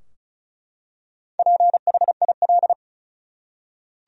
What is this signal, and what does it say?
Morse code, heard as a keyed tone, about 36 wpm, 700 Hz. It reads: PHIL
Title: USA Common Names - Max 5 Letters - 35wpm